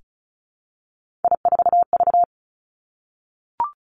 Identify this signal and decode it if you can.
Morse code, heard as a keyed tone, about 36 wpm, 700 Hz. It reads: I4V
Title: Three Challenging Letter-Number Combinations 35wpm